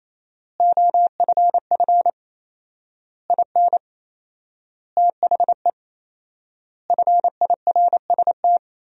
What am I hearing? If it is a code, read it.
Morse code, heard as a keyed tone, about 28 wpm, 700 Hz. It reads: OFF IN THE FIRST
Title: 4x Word-Spacing - Sentences from Top 200 Words - 28wpm